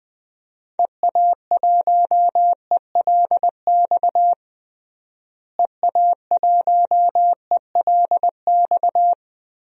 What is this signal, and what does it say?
Morse code, heard as a keyed tone, about 20 wpm, 700 Hz. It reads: EA1ELX EA1ELX
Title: DXCC-100 Call Signs - 6 Characters - Repeated 2x - 20wpm